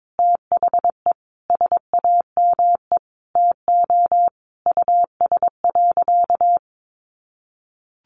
Morse code, heard as a keyed tone, about 22 wpm, 700 Hz. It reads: THE SAME TO US.
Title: Sentences from Top 600 Words 22wpm